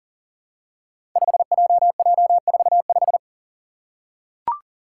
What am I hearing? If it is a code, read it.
Morse code, heard as a keyed tone, about 40 wpm, 690 Hz. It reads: 5JJ45
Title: Five Challenging Letter-Number Combinations 40wpm